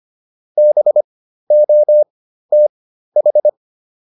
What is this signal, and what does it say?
Morse code, heard as a keyed tone, about 25 wpm, spaced as for 13 wpm, 600 Hz. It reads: BOTH